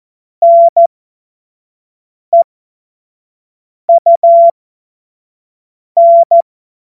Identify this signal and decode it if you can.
Morse code, heard as a keyed tone, about 14 wpm, spaced as for 5 wpm, 680 Hz. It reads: NEUN
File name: Morsekurs / Witze / Witz 05. April 2022 25 F70